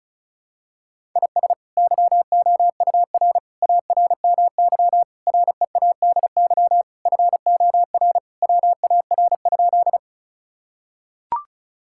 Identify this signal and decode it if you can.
Morse code, heard as a keyed tone, about 36 wpm, 700 Hz. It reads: IS YOUR ARMY READY FOR WAR?
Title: Sentences from Top 1000 Words - Review Part 5 - 35wpm